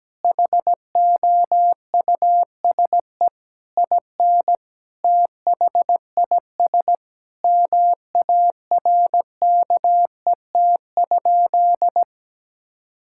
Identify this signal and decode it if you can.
Morse code, heard as a keyed tone, about 17 wpm, 700 Hz. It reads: HOUSE IN THIS MARKET?
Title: Sentences from Top 1000 Words - Review Part 5 - 17wpm